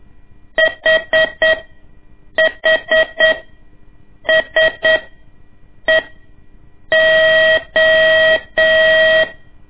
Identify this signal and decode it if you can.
Morse code, heard as a keyed tone, about 8 wpm, spaced as for 6 wpm, 640 Hz. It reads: HHSEO
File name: ik1weg2 / CORSI CW / lez2-1